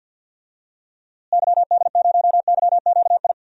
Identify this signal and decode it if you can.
Morse code, heard as a keyed tone, about 50 wpm, 700 Hz. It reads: YD0YQI
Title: DXCC-100 Call Signs - 6 Characters - Repeated 2x - 50wpm